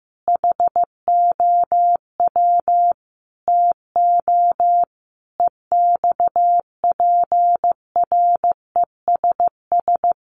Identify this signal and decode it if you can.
Morse code, heard as a keyed tone, about 15 wpm, 700 Hz. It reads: HOW TO EXPRESS
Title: Sentences from Top 1100 Words 15wpm